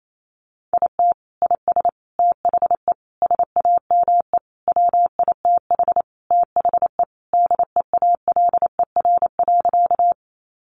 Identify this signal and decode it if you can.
Morse code, heard as a keyed tone, about 28 wpm, 700 Hz. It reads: IT IS THE SAME WITH THE DEALER.
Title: Sentences from Top 1100 Words 28wpm